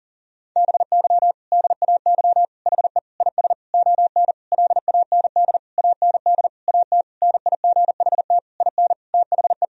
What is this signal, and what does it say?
Morse code, heard as a keyed tone, about 37 wpm, 700 Hz. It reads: BY DAY HE IS ON LAND AND AT NIGHT IN THE
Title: Sentences from Top 1000 Words - Review Part 1 - 40wpm